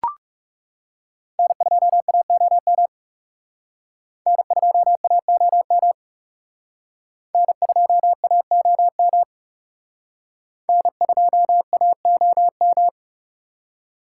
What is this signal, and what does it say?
Morse code, heard as a keyed tone, about 37 wpm, 700 Hz. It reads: N2AOM N2AOM N2AOM N2AOM
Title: Speed Racing - USA Call Signs 30wpm